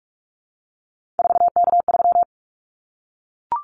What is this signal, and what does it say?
Morse code, heard as a keyed tone, about 44 wpm, 700 Hz. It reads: 4K3
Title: Callsign Prefixes for Top 100 DXCC - 45wpm